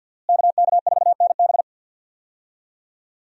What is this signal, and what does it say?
Morse code, heard as a keyed tone, about 52 wpm, 700 Hz. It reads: KK4NB
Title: USA Call Signs - 5 Characters - Repeated 3x - 50wpm